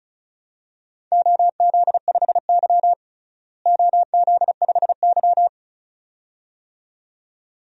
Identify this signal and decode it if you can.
Morse code, heard as a keyed tone, about 34 wpm, 700 Hz. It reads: OZ5Y OZ5Y
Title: DXCC-100 Call Signs - 4 Characters - Repeated 2x - 35wpm